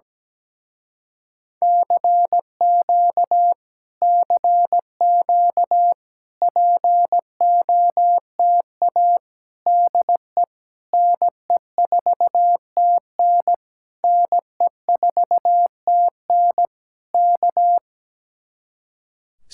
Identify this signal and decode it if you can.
Morse code, heard as a keyed tone, about 17 wpm, 700 Hz. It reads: CQ CQ POTA DE NE4TN NE4TN K
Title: Speed-Racing Common QSO Phrases - 17wpm